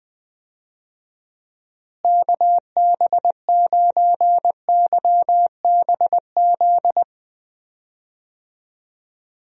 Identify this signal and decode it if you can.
Morse code, heard as a keyed tone, about 20 wpm, 700 Hz. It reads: KB9YBZ
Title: USA Call Signs - 6 Characters - Repeated 3x - 20wpm